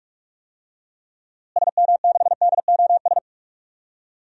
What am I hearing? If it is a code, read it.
Morse code, heard as a keyed tone, about 42 wpm, 700 Hz. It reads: SM6DOS
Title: DXCC-100 Call Signs - 6 Characters - Repeated 2x - 45wpm